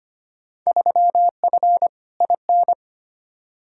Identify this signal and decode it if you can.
Morse code, heard as a keyed tone, about 25 wpm, 700 Hz. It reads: 3F IN
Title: Field Day Exchanges - 25wpm